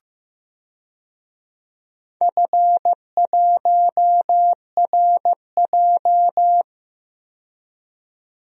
Morse code, heard as a keyed tone, about 15 wpm, 700 Hz. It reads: F1RJ
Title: DXCC-100 Call Signs - 4 Characters - 15wpm